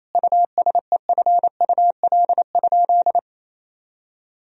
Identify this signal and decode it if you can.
Morse code, heard as a keyed tone, about 28 wpm, 700 Hz. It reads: USEFUL?
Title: Sentences from Top 1000 Words - Review Part 4 - 28wpm